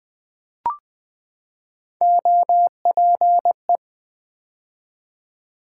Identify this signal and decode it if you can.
Morse code, heard as a keyed tone, about 20 wpm, 700 Hz. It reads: OPE
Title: Build A Word - 3 Letters At A Time - 6-Letter Words - 20wpm